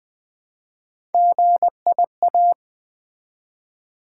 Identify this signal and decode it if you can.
Morse code, heard as a keyed tone, about 20 wpm, 700 Hz. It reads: GIA
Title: Build A Word - 1 Letter At A Time - 5-Letter Words - 20wpm